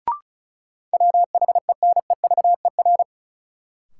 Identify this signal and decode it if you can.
Morse code, heard as a keyed tone, about 34 wpm, 700 Hz. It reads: WHENEVER
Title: Sentences from Top 1200 Words - New Word Drill - 35wpm